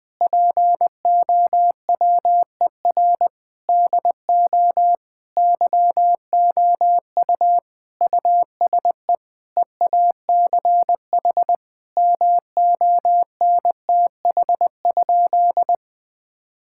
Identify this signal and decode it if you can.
Morse code, heard as a keyed tone, about 20 wpm, 700 Hz. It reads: POWER DO YOU USE EACH MONTH?
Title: Sentences from Top 1000 Words - Review Part 6 - 20wpm